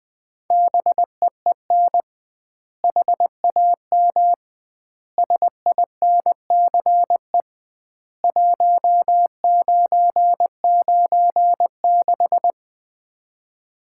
Morse code, heard as a keyed tone, about 20 wpm, 700 Hz. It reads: BEEN HAM SINCE 1996
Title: Common QSO Phrases - 2x Word-Spacing - 20wpm